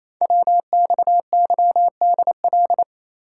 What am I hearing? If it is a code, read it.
Morse code, heard as a keyed tone, about 27 wpm, 700 Hz. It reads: WXYDL
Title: Code Groups - 5 Letters - 28wpm